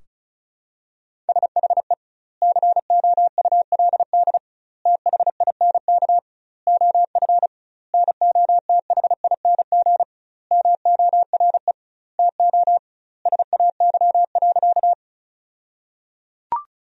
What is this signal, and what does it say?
Morse code, heard as a keyed tone, about 34 wpm, 700 Hz. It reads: SHE COULD THINK OF NOTHING MORE TO SAY.
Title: 2x Word-Spacing - Sentences from Top 500 Words - 35wpm